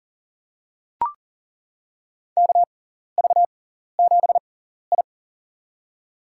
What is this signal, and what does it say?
Morse code, heard as a keyed tone, about 40 wpm, spaced as for 14 wpm, 700 Hz. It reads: KV7I
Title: USA Call Signs - 4 Characters - 6x Character Spacing - 40wpm